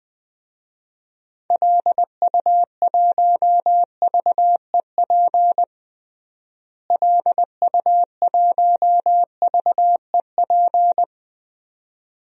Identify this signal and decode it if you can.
Morse code, heard as a keyed tone, about 20 wpm, 700 Hz. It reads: LU1VEP LU1VEP
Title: DXCC-100 Call Signs - 6 Characters - Repeated 2x - 20wpm